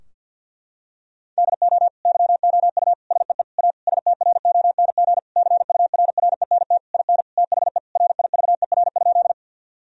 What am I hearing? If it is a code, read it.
Morse code, heard as a keyed tone, about 50 wpm, 700 Hz. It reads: DO YOU SEE A STRONG CURRENT IN THE RIVER?
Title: Sentences from Top 1000 Words - Review Part 4 - 50wpm